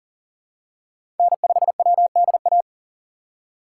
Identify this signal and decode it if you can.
Morse code, heard as a keyed tone, about 40 wpm, 700 Hz. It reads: N5WDA
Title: USA Call Signs - 4x Character Spacing - 40wpm